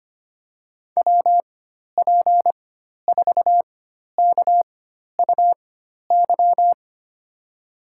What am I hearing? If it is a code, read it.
Morse code, heard as a keyed tone, about 25 wpm, spaced as for 12 wpm, 700 Hz. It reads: WP4KUY
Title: USA Call Signs - 6 Characters - 4x Character Spacing - 25wpm